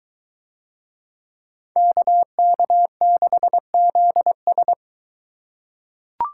Morse code, heard as a keyed tone, about 23 wpm, 700 Hz. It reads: KK6ZS